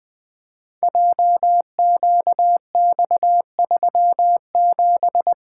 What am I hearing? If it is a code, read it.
Morse code, heard as a keyed tone, about 20 wpm, 700 Hz. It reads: JQX37
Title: Code Groups - 5 Letter:Numbers - 20wpm